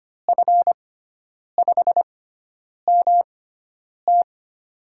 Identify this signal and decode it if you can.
Morse code, heard as a keyed tone, about 25 wpm, spaced as for 9 wpm, 700 Hz. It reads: F5MT
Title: USA Call Signs - 5 Characters - 6x Character Spacing - 25wpm